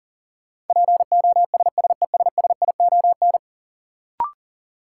Morse code, heard as a keyed tone, about 40 wpm, 700 Hz. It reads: POSSESSION
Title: Sentences from Top 1400 Words - New Word Drill Alt - 40wpm